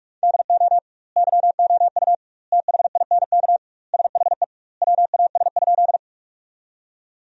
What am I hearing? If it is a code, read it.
Morse code, heard as a keyed tone, about 46 wpm, 700 Hz. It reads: DO YOU THINK SHE WAS?
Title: 2x Word-Spacing - Sentences from Top 200 Words - 45wpm